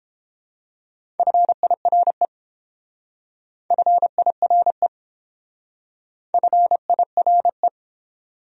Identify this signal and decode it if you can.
Morse code, heard as a keyed tone, about 29 wpm, 700 Hz. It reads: FIRE FIRE FIRE